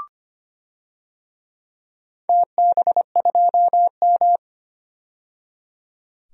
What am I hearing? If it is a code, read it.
Morse code, heard as a keyed tone, about 25 wpm, 700 Hz. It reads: TB2M
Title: DXCC-100 Call Signs - 4 Characters - 25wpm